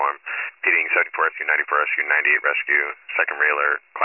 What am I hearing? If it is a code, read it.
Emergency Alert System (SAME) tones, heard as harsh buzzing bursts, with speech. It is heard from 0.6 to 3.5 s.